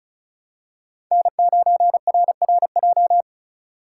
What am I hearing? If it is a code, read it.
Morse code, heard as a keyed tone, about 36 wpm, 700 Hz. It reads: N9RRJ
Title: USA Call Signs - 5 Characters - 4x Character Spacing - 35wpm